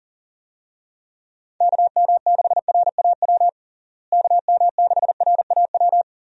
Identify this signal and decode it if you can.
Morse code, heard as a keyed tone, about 40 wpm, 700 Hz. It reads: KM6RAW KM6RAW
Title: USA Call Signs - 6 Characters - Repeated 3x - 40wpm